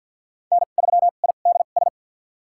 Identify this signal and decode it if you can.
Morse code, heard as a keyed tone, about 50 wpm, spaced as for 36 wpm, 700 Hz. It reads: N3IDS
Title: USA Call Signs - 5 Characters - 2x Character Spacing - 50wpm